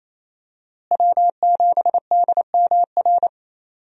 Morse code, heard as a keyed tone, about 28 wpm, 700 Hz. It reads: W7DMR
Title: USA Call Signs - 5 Characters - Repeated 3x - 28wpm